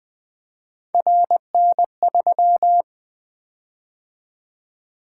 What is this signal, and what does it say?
Morse code, heard as a keyed tone, about 20 wpm, 700 Hz. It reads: RN3
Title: Three Letter-Number Combinations - 20wpm